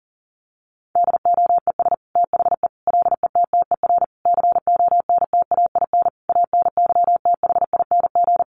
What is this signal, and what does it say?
Morse code, heard as a keyed tone, about 40 wpm, 700 Hz. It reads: DOES THE LETTER CONTAIN ANYTHING
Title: Sentences from Top 1100 Words 40wpm